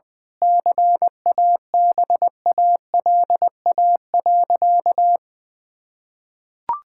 Morse code, heard as a keyed tone, about 20 wpm, 700 Hz. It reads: CABALA.
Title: Pangram Challenge v2 - 20wpm